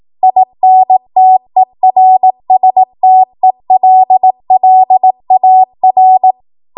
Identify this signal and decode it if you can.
Morse code, heard as a keyed tone, about 18 wpm, 750 Hz. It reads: INTERSTELLAR
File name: Morse / 18WPM